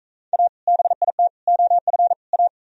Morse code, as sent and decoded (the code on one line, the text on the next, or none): .- / -... .. - / --- ..-. / .-
A BIT OF A